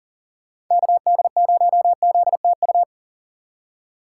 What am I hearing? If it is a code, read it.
Morse code, heard as a keyed tone, about 39 wpm, 700 Hz. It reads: KD0ZTU
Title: USA Call Signs - 6 Characters - 3x Character Spacing - 40wpm